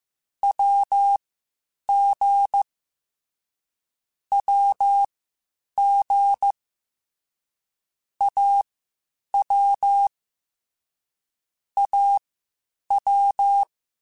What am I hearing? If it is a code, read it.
Morse code, heard as a keyed tone, about 15 wpm, spaced as for 8 wpm, 800 Hz. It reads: WG WG AW AW
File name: Short Runs / Short Runs-mp3s / Unballanced @13